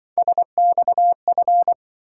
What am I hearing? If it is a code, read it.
Morse code, heard as a keyed tone, about 23 wpm, 700 Hz. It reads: SXF
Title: Speed-Racing - Three Letter-Number Combinations - 20wpm